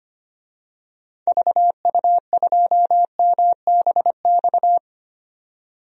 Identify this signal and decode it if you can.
Morse code, heard as a keyed tone, about 25 wpm, 700 Hz. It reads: VU2MBX
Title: DXCC-100 Call Signs - 6 Characters - 25wpm